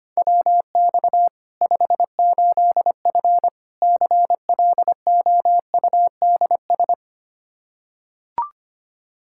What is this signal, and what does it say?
Morse code, heard as a keyed tone, about 25 wpm, 700 Hz. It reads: WX 58F CLOUDS
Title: Practice QSO - Exchange 2 - Info - 25wpm